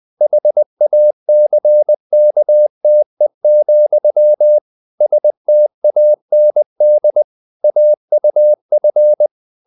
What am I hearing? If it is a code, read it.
Morse code, heard as a keyed tone, about 20 wpm, 600 Hz. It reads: HACKTE, STAND AUF